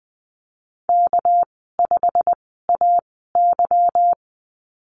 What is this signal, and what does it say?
Morse code, heard as a keyed tone, about 20 wpm, 700 Hz. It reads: K 5 A Y
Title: 238 Lesson 38g - Letter q - Call Signs 20wpm